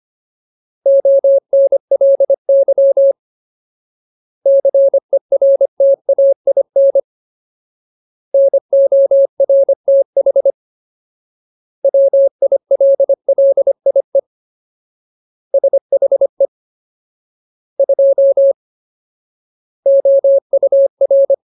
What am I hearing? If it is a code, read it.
Morse code, heard as a keyed tone, about 25 wpm, 550 Hz. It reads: ONLY CERTAIN NORTH WILLIE SHE 2 OUR
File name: Ultimate Word List / 25WPM Ultimate